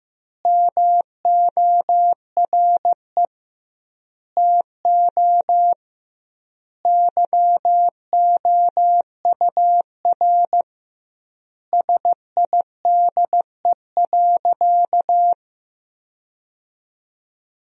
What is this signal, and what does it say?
Morse code, heard as a keyed tone, about 15 wpm, 700 Hz. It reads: MORE TO YOUR SIDE.
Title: 2x Word-Spacing - Sentences from Top 400 Words - 15wpm